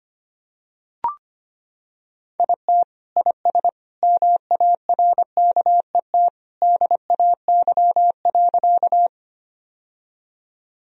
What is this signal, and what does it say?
Morse code, heard as a keyed tone, about 25 wpm, 700 Hz. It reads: IT IS MARKET DAY.
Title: Sentences from Top 1000 Words - Review Part 3 - 25wpm